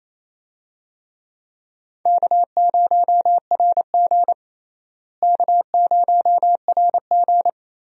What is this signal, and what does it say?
Morse code, heard as a keyed tone, about 28 wpm, 700 Hz. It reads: K0RG K0RG
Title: USA Call Signs - 4 Characters - Repeated 2x - 28wpm